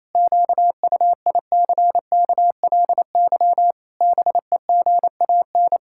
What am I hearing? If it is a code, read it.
Morse code, heard as a keyed tone, about 28 wpm, 700 Hz. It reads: QUICKLY BEGAN
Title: Pangram Challenge v2 - 28wpm